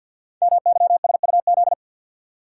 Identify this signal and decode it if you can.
Morse code, heard as a keyed tone, about 50 wpm, 700 Hz. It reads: MYSUZ
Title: Code Groups - 5 Letters - 50wpm